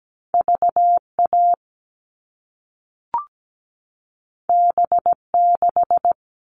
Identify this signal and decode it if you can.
Morse code, heard as a keyed tone, about 17 wpm, 700 Hz. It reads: VA B6
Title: All Cut Numbers - 2-Digit 17wpm